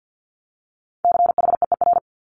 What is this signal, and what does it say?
Morse code, heard as a keyed tone, about 50 wpm, 690 Hz. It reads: CHEER
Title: Multi-letter Phonemes Repeated 3x 50wpm